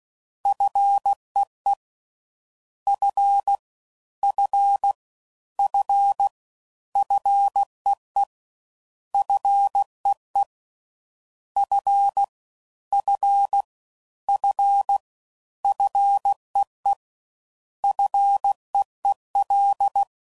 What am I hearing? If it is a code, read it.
Morse code, heard as a keyed tone, about 16 wpm, 800 Hz. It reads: FEE F F F FEE FEE F F F FEE FEEL